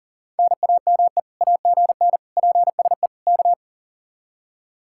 Morse code, heard as a keyed tone, about 51 wpm, 700 Hz. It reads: NAME AGN PSE K